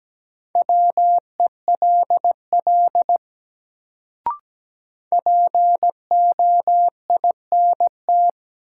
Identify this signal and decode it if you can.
Morse code, heard as a keyed tone, about 17 wpm, 700 Hz. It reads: WELL POINT
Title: Sentences from Top 200 Words - New Word Drill - 17wpm